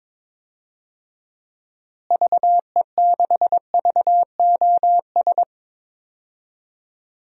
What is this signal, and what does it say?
Morse code, heard as a keyed tone, about 22 wpm, 700 Hz. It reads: VE6VOS
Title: DXCC-100 Call Signs - 6 Characters - Repeated 2x - 22wpm